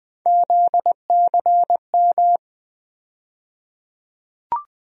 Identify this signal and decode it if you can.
Morse code, heard as a keyed tone, about 20 wpm, 700 Hz. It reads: ZCM
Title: USA Call Sign Postfixes - 3 Letters 20wpm